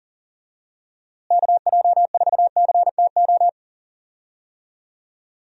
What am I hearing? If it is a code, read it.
Morse code, heard as a keyed tone, about 39 wpm, 700 Hz. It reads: KJ4CTO
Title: USA Call Signs - 6 Characters - Repeated 2x - 40wpm